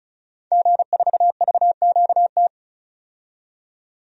Morse code, heard as a keyed tone, about 34 wpm, 700 Hz. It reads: G4VQT
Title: DXCC-100 Call Signs - 5 Characters - 35wpm